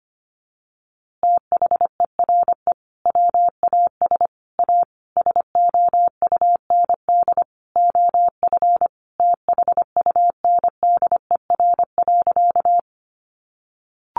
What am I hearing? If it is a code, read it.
Morse code, heard as a keyed tone, about 25 wpm, 700 Hz. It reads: THERE WAS A SOUND OF THUNDER.
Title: A Sound Of Thunder 25wpm